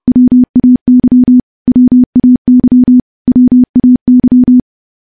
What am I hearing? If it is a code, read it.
Morse code, heard as a keyed tone, about 30 wpm, 250 Hz. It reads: WAY WAY WAY
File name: morse-training-30WPM